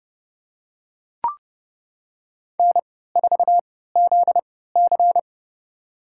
Morse code, heard as a keyed tone, about 31 wpm, spaced as for 17 wpm, 700 Hz. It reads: N4ZC
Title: USA Call Signs - 4 Characters - 3x Character Spacing - 30wpm